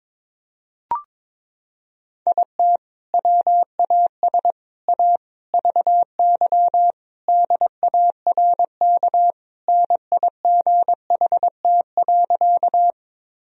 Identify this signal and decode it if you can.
Morse code, heard as a keyed tone, about 22 wpm, 700 Hz. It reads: IT WAS A VY DARK NIGHT.
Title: Sentences from Top 500 Words and Abbreviations 22wpm